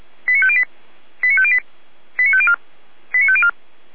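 Five-tone selective call (ZVEI), heard as quick runs of five tones, tones of about 70 ms. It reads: 89508 89508 89504 89504
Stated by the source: ZVEI five-tone call received on SDR, feuerwehr-orlamuende.org fire brigade.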